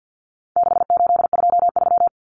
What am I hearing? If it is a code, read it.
Morse code, heard as a keyed tone, about 50 wpm, 700 Hz. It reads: 6823
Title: Numbers - 4-Digit 50wpm